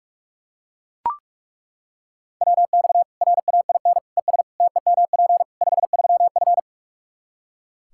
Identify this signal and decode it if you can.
Morse code, heard as a keyed tone, about 42 wpm, 700 Hz. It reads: WX RAIN ES TEMP 53F
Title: Practice QSO - Exchange 2 - Info - 45wpm